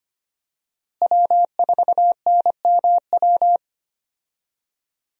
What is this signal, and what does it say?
Morse code, heard as a keyed tone, about 25 wpm, 700 Hz. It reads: W4NMW
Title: USA Call Signs - 5 Characters - Repeated 3x - 25wpm